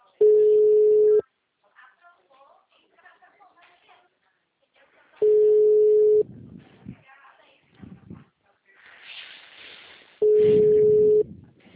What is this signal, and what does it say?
Ringback tone (Europe, 425 Hz), with speech.